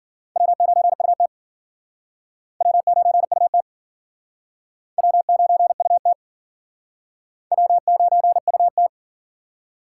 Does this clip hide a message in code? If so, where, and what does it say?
Yes: Morse code from 0.4 to 8.9 s, heard as a keyed tone, about 53 wpm, 700 Hz. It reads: W9UT W9UT W9UT W9UT